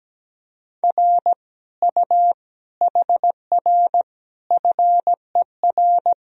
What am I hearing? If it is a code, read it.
Morse code, heard as a keyed tone, about 17 wpm, 700 Hz. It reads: R U HR FER
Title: Sentences from Top 200 Words and Abbreviations 17wpm